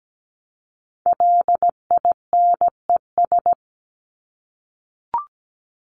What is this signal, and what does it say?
Morse code, heard as a keyed tone, about 17 wpm, 700 Hz. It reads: LINES
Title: ES:S Suffix - Top 100 Words with Base Words 17wpm